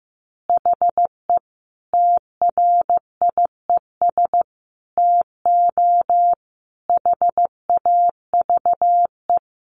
Morse code, heard as a keyed tone, about 15 wpm, 700 Hz. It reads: HE TRIES TO HAVE
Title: Easy Phrases 15wpm